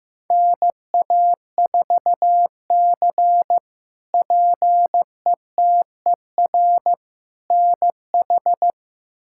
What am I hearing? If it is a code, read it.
Morse code, heard as a keyed tone, about 16 wpm, 700 Hz. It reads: NA4C PETER NH
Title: K1USN SST QSO Practice - 15wpm